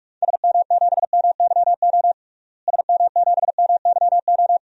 Morse code, heard as a keyed tone, about 46 wpm, 700 Hz. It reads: SM7MYO SM7MYO
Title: DXCC-100 Call Signs - 6 Characters - Repeated 2x - 45wpm